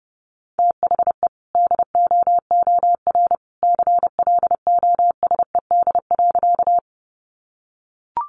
Morse code, heard as a keyed tone, about 30 wpm, 700 Hz. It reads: THE DOOR CLOSED.
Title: Sentences from Top 1400 Words 30wpm